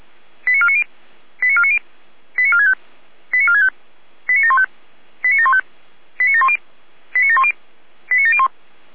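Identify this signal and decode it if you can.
Five-tone selective call (ZVEI), heard as quick runs of five tones, tones of about 70 ms. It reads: 89490 89490 89465 89465 89715 89715 89710 89710 89701
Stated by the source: ZVEI five-tone selective call received with an SDR.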